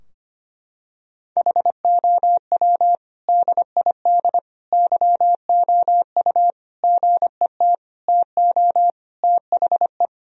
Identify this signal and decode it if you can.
Morse code, heard as a keyed tone, about 25 wpm, 700 Hz. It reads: HOW DID YOU GET TO THE
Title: Sentences from Top 1000 Words - Review Part 3 - 25wpm